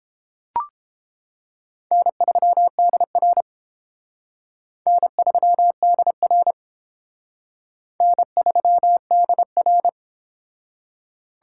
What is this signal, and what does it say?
Morse code, heard as a keyed tone, about 31 wpm, 700 Hz. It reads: N3DR N3DR N3DR